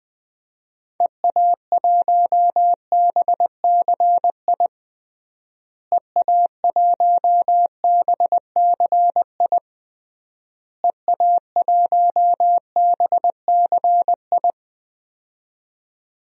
Morse code, heard as a keyed tone, about 21 wpm, 700 Hz. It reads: EA1BCI EA1BCI EA1BCI
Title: DXCC-100 Call Signs - 6 Characters - Repeated 3x - 20wpm